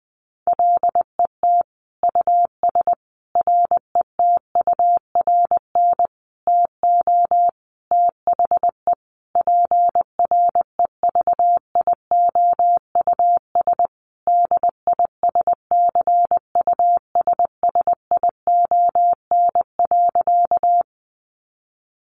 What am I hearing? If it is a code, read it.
Morse code, heard as a keyed tone, about 20 wpm, 700 Hz. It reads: LET US RETURN TO THE PREVIOUS DISCUSSION.
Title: Sentences from Top 1500 Words 20wpm